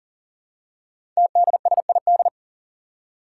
Morse code, heard as a keyed tone, about 40 wpm, 700 Hz. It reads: TDSID
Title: Five Challenging Letter-Number Combinations 40wpm